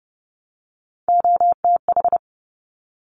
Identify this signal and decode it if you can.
Morse code, heard as a keyed tone, about 30 wpm, 690 Hz. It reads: OTH
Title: Top Three Letter Combinations in Words 30wpm